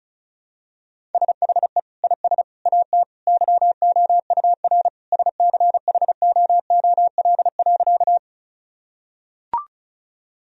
Morse code, heard as a keyed tone, about 34 wpm, 700 Hz. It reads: SHE IS AT YOUR SCHOOL.
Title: Sentences from Top 1000 Words - Review Part 1 - 35wpm